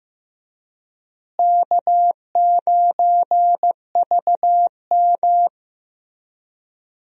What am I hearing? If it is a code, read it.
Morse code, heard as a keyed tone, about 15 wpm, 700 Hz. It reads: K9VM
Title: USA Call Signs - 4 Characters - Repeated 3x - 15wpm